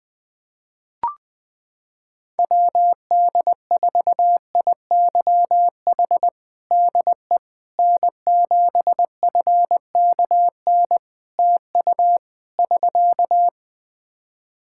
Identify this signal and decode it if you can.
Morse code, heard as a keyed tone, about 20 wpm, 700 Hz. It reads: WD4IYH DE N7FKN TU <SK>
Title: Practice QSO - Exchange 4 - Ending - 20wpm